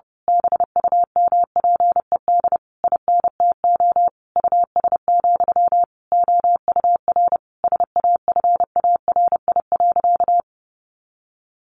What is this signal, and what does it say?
Morse code, heard as a keyed tone, about 30 wpm, 700 Hz. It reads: BUMPED INTO US, OUR SAFARI.
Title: A Sound Of Thunder 30wpm